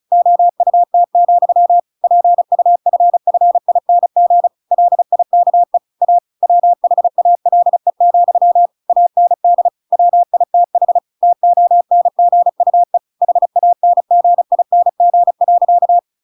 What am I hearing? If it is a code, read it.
Morse code, heard as a keyed tone, about 36 wpm, 700 Hz. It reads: OUT, PUFFING LIKE A WHALE, AND WITH TONGUE HANGING.
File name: Pinocchio / Pinocchio ch6 35 35 ik1hge